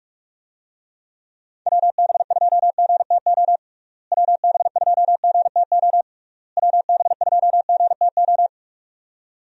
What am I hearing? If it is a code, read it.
Morse code, heard as a keyed tone, about 46 wpm, 700 Hz. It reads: WB2GTO WB2GTO WB2GTO
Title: USA Call Signs - 6 Characters - Repeated 3x - 45wpm